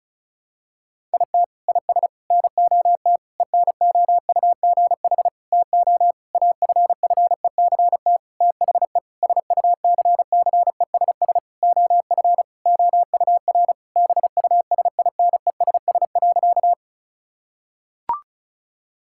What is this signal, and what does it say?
Morse code, heard as a keyed tone, about 34 wpm, 700 Hz. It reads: IT IS NOT ENOUGH TO AFFECT THE SUCCESS OF OUR BUSINESS.
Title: Sentences from Top 1000 Words - Review Part 6 - 35wpm